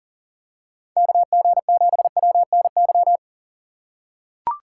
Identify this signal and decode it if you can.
Morse code, heard as a keyed tone, about 40 wpm, 700 Hz. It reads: KG7WNY
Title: USA Call Signs - 6 Characters - 4x Character Spacing - 40wpm